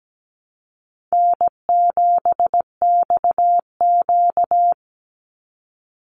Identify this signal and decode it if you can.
Morse code, heard as a keyed tone, about 17 wpm, 700 Hz. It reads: N7XQ
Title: USA Call Signs 17wpm